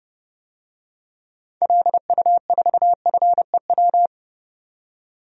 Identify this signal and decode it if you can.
Morse code, heard as a keyed tone, about 32 wpm, 700 Hz. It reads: LU4FEW
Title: DXCC-100 Call Signs - 6 Characters - 30wpm